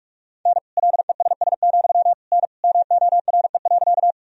Morse code, heard as a keyed tone, about 46 wpm, 700 Hz. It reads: N LESS, N MORE.
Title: Sentences from Top 400 Words and Abbreviations 45wpm